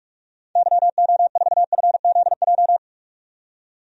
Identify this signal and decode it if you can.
Morse code, heard as a keyed tone, about 44 wpm, 700 Hz. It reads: YO4FZJ
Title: DXCC-100 Call Signs - 6 Characters - 45wpm